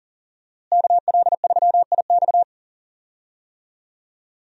Morse code, heard as a keyed tone, about 40 wpm, 700 Hz. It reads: KL3IX
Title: USA Call Signs - 5 Characters - Repeated 3x - 40wpm